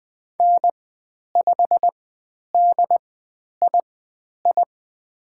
Morse code, heard as a keyed tone, about 20 wpm, spaced as for 10 wpm, 700 Hz. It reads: N5DII
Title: 102 Lesson 17g - Letter u - Call Signs 20wpm